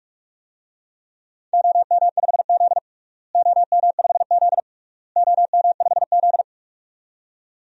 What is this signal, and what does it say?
Morse code, heard as a keyed tone, about 47 wpm, 700 Hz. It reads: OM5Z OM5Z OM5Z
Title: DXCC-100 Call Signs - 4 Characters - Repeated 3x - 45wpm